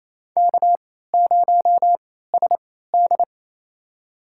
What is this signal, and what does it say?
Morse code, heard as a keyed tone, about 27 wpm, spaced as for 16 wpm, 700 Hz. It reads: K0SD
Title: USA Call Signs - 4 Characters - 3x Character Spacing - 28wpm